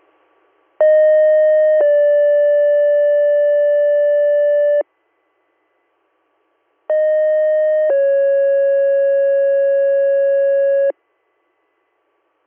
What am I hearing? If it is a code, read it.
Two-tone page, heard as a one-second tone, then a three-second tone, about 620 then 580 Hz, about 620 then 550 Hz. It is heard from 0.8 to 10.9 s.